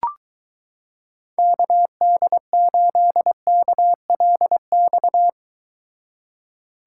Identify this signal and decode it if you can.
Morse code, heard as a keyed tone, about 23 wpm, 700 Hz. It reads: KD8KLX